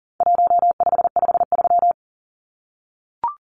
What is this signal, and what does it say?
Morse code, heard as a keyed tone, about 37 wpm, 700 Hz. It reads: 1553
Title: Numbers - 4-Digit 40wpm